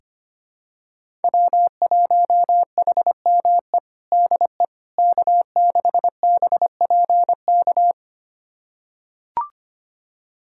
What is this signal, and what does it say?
Morse code, heard as a keyed tone, about 25 wpm, 700 Hz. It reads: W1HME DE K6BPK
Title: USA Callsign DE Callsign - 5 Characters - 25wpm